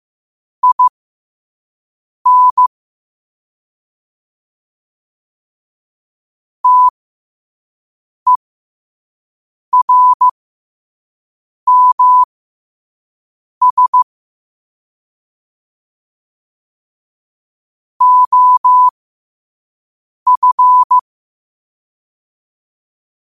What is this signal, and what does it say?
Morse code, heard as a keyed tone, about 15 wpm, spaced as for 5 wpm, 1000 Hz. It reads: IN TERMS OF
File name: qotdfiles / 5wpm